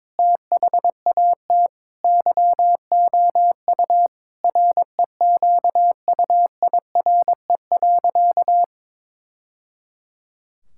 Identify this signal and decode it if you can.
Morse code, heard as a keyed tone, about 22 wpm, 700 Hz. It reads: THAT YOU REQUIRE.
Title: Sentences from Top 1000 Words - Review Part 6 - 22wpm